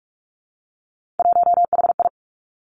Morse code, heard as a keyed tone, about 46 wpm, 700 Hz. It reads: 1HI
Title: Three Challenging Letter-Number Combinations 45wpm